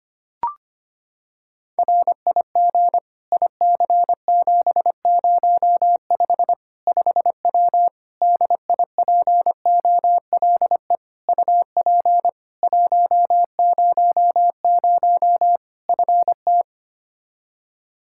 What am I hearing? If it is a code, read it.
Morse code, heard as a keyed tone, about 25 wpm, 700 Hz. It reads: RIG IC705 5W DIPOLE UP 100 FT
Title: K1USN SST QSO Practice 25wpm